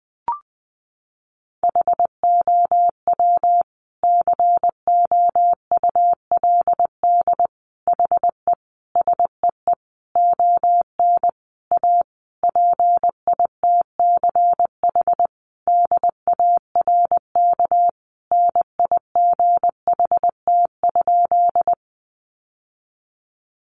Morse code, heard as a keyed tone, about 20 wpm, 700 Hz. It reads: HOW COULD HE SEE ON A PITCH DARK NIGHT?
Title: Sentences from Top 1300 Words 20wpm